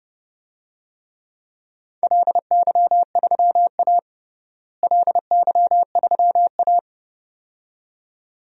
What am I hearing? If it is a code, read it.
Morse code, heard as a keyed tone, about 30 wpm, 700 Hz. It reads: LY3A LY3A
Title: DXCC-100 Call Signs - 4 Characters - Repeated 2x - 30wpm